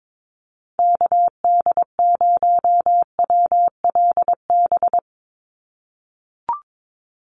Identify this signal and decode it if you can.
Morse code, heard as a keyed tone, about 22 wpm, 700 Hz. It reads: KD0WLB
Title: USA Call Signs 22wpm